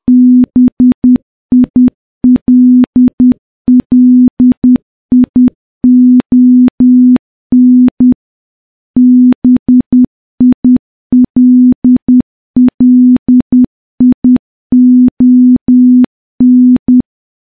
Morse code, heard as a keyed tone, about 10 wpm, 250 Hz. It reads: BILLION BILLION
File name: morse-training-10WPM